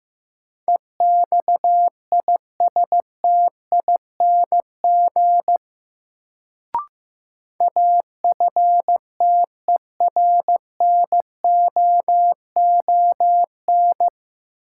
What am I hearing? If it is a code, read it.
Morse code, heard as a keyed tone, about 15 wpm, 700 Hz. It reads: EXISTING AFTERNOON
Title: Sentences from Top 1500 Words - New Word Drill - 15wpm